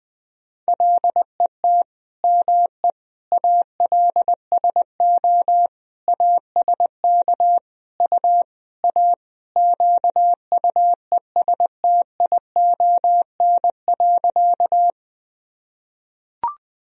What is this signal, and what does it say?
Morse code, heard as a keyed tone, about 21 wpm, 700 Hz. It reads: LET ME ALSO ASK U A QUESTION.
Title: Sentences from Top 400 Words and Abbreviations 20wpm